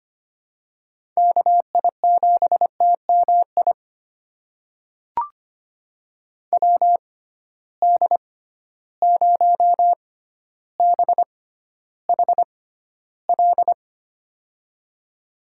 Morse code, heard as a keyed tone, about 25 wpm, spaced as for 13 wpm, 700 Hz. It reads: KI7TMI WD0BHL
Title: USA Call Signs - 6 Characters - 6x Character Spacing - 25wpm